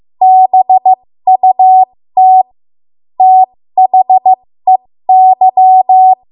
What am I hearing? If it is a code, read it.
Morse code, heard as a keyed tone, about 15 wpm, spaced as for 13 wpm, 750 Hz. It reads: BUT THEY
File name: Morse / 13WPM